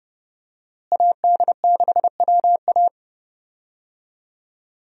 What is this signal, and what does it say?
Morse code, heard as a keyed tone, about 29 wpm, 700 Hz. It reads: AD6WA
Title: USA Call Signs - 5 Characters - Repeated 3x - 30wpm